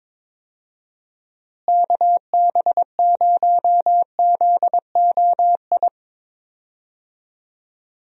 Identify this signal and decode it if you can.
Morse code, heard as a keyed tone, about 22 wpm, 700 Hz. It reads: KB0ZOI
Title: USA Call Signs - 6 Characters - Repeated 2x - 22wpm